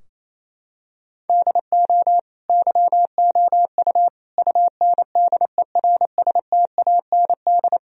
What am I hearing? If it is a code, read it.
Morse code, heard as a keyed tone, about 27 wpm, 700 Hz. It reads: DO YOU UNDERSTAND
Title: Sentences from Top 1000 Words - Review Part 3 - 28wpm